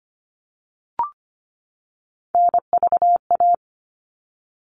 Morse code, heard as a keyed tone, about 25 wpm, 700 Hz. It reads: NVA
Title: All Cut Numbers - 3-Digit 25wpm